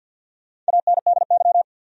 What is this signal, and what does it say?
Morse code, heard as a keyed tone, about 52 wpm, 700 Hz. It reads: ANDY
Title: USA Common Names - Max 5 Letters - 50wpm